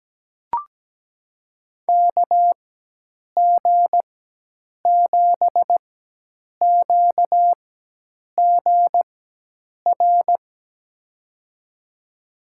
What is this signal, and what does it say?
Morse code, heard as a keyed tone, about 17 wpm, spaced as for 8 wpm, 700 Hz. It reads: KG7QGR
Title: USA Call Signs - 6 Characters - 4x Character Spacing - 17wpm